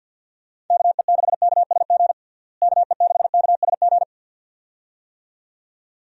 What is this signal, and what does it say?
Morse code, heard as a keyed tone, about 47 wpm, 700 Hz. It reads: KE6KSG KE6KSG
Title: USA Call Signs - 6 Characters - Repeated 2x - 50wpm